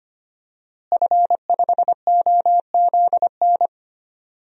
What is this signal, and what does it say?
Morse code, heard as a keyed tone, about 25 wpm, 700 Hz. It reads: F5OZN
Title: DXCC-100 Call Signs - 5 Characters - Repeated 3x - 25wpm